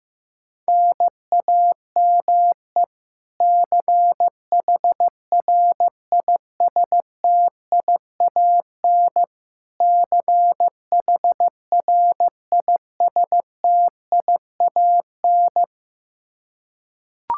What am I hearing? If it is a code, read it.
Morse code, heard as a keyed tone, about 15 wpm, 700 Hz. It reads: NAME CHRISTIAN CHRISTIAN
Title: Practice QSO - Exchange 1 - Beginning - 15wpm